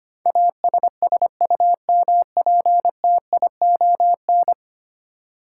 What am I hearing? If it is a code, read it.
Morse code, heard as a keyed tone, about 25 wpm, 700 Hz. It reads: ASSUMPTION
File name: Ten Letter Words - 25wpm